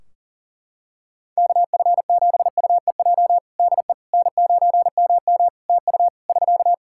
Morse code, heard as a keyed tone, about 40 wpm, 700 Hz. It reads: KF7UEJ DE N9MM TU <SK>
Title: Practice QSO - Exchange 4 - Ending - 40wpm